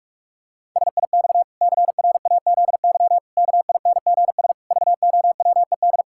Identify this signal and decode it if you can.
Morse code, heard as a keyed tone, about 46 wpm, 700 Hz. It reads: SIX CRAZY KINGS VOWED
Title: Pangram Challenge v2 - 45wpm